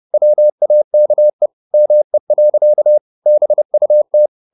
Morse code, heard as a keyed tone, about 30 wpm, 600 Hz. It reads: WAKE ME. BUT
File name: the old man and the sea / the old man and the sea 1 30wpm